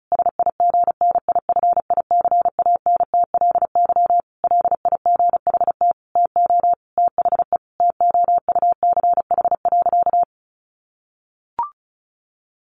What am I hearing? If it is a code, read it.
Morse code, heard as a keyed tone, about 34 wpm, 700 Hz. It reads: SIGNIFICANTLY LIGHT TO THE TOUCH.
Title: Sentences from Top 1500 Words 35wpm